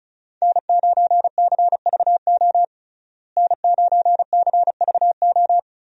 Morse code, heard as a keyed tone, about 34 wpm, 700 Hz. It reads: N9CVO N9CVO
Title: USA Call Signs - 5 Characters - Repeated 3x - 35wpm